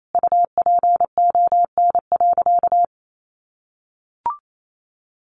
Morse code, heard as a keyed tone, about 28 wpm, 700 Hz. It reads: UPON.
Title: Sentences from Top 1100 Words 28wpm